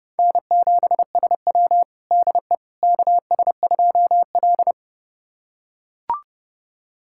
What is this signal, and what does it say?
Morse code, heard as a keyed tone, about 32 wpm, 700 Hz. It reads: N7SW DE KS2L
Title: USA Callsign DE Callsign - 4 Characters - 30wpm